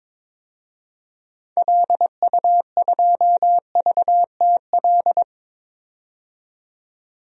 Morse code, heard as a keyed tone, about 22 wpm, 700 Hz. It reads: LU2VTL
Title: DXCC-100 Call Signs - 6 Characters - 22wpm